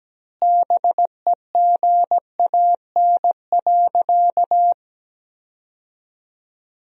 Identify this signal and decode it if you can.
Morse code, heard as a keyed tone, about 17 wpm, 700 Hz. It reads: BEGAN.
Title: 3x Word-Spacing - Sentences from Top 300 Words - 17wpm